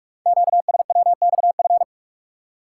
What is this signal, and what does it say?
Morse code, heard as a keyed tone, about 44 wpm, 700 Hz. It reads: QSWXF
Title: Code Groups - 5 Letters - 45wpm